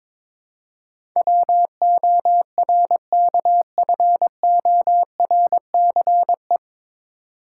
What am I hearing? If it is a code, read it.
Morse code, heard as a keyed tone, about 22 wpm, 700 Hz. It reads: WORKFORCE
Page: Morse Code Ninja practice 22wpm